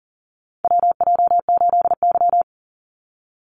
Morse code, heard as a keyed tone, about 40 wpm, 700 Hz. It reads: WJ8Y
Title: Call Signs Repeated 2x - 40wpm